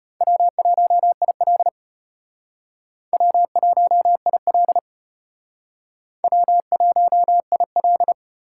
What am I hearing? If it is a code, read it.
Morse code, heard as a keyed tone, about 32 wpm, 700 Hz. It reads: W1IL W1IL W1IL